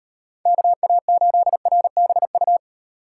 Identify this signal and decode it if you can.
Morse code, heard as a keyed tone, about 38 wpm, 700 Hz. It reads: KA8RBU